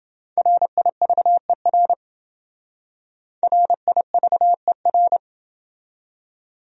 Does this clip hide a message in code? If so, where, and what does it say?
Yes: Morse code from 0.4 to 5.2 s, heard as a keyed tone, about 29 wpm, 700 Hz. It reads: RIVER RIVER